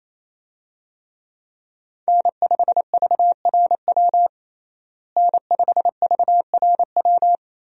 Morse code, heard as a keyed tone, about 28 wpm, 700 Hz. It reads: N5VRW N5VRW
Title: USA Call Signs - 5 Characters - Repeated 3x - 28wpm